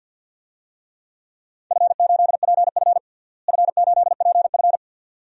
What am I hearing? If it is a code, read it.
Morse code, heard as a keyed tone, about 47 wpm, 700 Hz. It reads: F8PF F8PF
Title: DXCC-100 Call Signs - 4 Characters - Repeated 2x - 50wpm